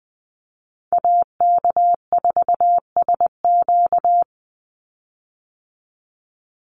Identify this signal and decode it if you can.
Morse code, heard as a keyed tone, about 20 wpm, 700 Hz. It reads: AK4SQ
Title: Call Signs Repeated 2x - 20wpm